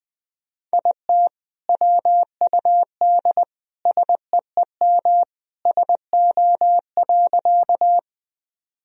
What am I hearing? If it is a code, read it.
Morse code, heard as a keyed tone, about 20 wpm, 700 Hz. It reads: IT WUD SEEM SO.
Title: Sentences from Top 300 Words and Abbreviations 20wpm